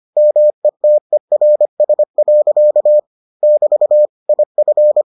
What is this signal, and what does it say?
Morse code, heard as a keyed tone, about 25 wpm, 600 Hz. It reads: METERS. <BT> IF